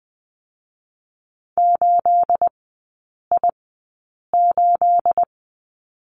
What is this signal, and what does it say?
Morse code, heard as a keyed tone, about 20 wpm, spaced as for 8 wpm, 690 Hz. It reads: 8I8
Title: 212 Lesson 34e - Number 8 - Three Characters 20wpm